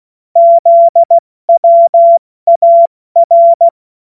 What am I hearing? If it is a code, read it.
Morse code, heard as a keyed tone, about 16 wpm, 680 Hz. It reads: ZWAR
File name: Morsekurs / Witze / Witz 07. Dezember 2021, 70 F80